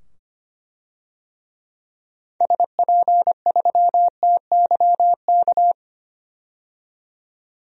Morse code, heard as a keyed tone, about 25 wpm, 700 Hz. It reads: SP3TYK
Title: DXCC-100 Call Signs - 6 Characters - 25wpm